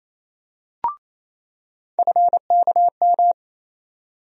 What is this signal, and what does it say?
Morse code, heard as a keyed tone, about 28 wpm, 700 Hz. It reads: FKM
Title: USA Call Sign Postfixes - 3 Letters 28wpm